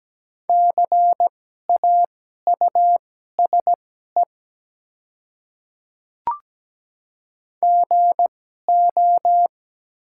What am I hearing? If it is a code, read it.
Morse code, heard as a keyed tone, about 17 wpm, spaced as for 12 wpm, 700 Hz. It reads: CAUSE GO
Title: 2x Character Spacing - Top 200 Words - 17wpm